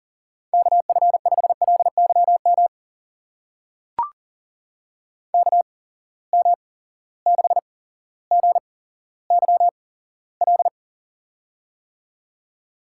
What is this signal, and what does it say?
Morse code, heard as a keyed tone, about 38 wpm, spaced as for 17 wpm, 700 Hz. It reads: KF5LYM KM6GYL
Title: USA Call Signs - 6 Characters - 8x Character Spacing - 40wpm